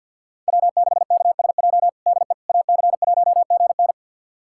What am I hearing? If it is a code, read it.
Morse code, heard as a keyed tone, about 52 wpm, 700 Hz. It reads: W6KSJ DE AC1GN
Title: USA Callsign DE Callsign - 5 Characters - 50wpm